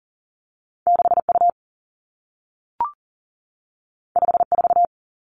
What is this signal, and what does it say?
Morse code, heard as a keyed tone, about 41 wpm, 700 Hz. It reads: 6U 54
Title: Two Challenging Letter-Number Combinations 40wpm